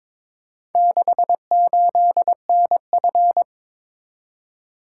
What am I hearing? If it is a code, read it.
Morse code, heard as a keyed tone, about 22 wpm, 700 Hz. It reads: 68NF
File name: practice / 22wpm / Four Letter-Number Combinations - 22wpm